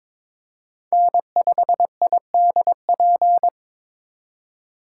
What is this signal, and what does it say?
Morse code, heard as a keyed tone, about 26 wpm, 700 Hz. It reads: N5IDP
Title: Speed Racing - USA 1x3 Call Signs 22wpm